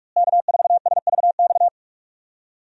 Morse code, heard as a keyed tone, about 45 wpm, 700 Hz. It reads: K4SVX
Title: USA Call Signs - 5 Characters - Repeated 3x - 45wpm